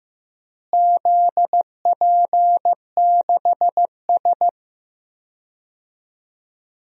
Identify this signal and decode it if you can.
Morse code, heard as a keyed tone, about 15 wpm, 700 Hz. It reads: ZP6S
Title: DXCC-100 Call Signs - 4 Characters - Repeated 3x - 15wpm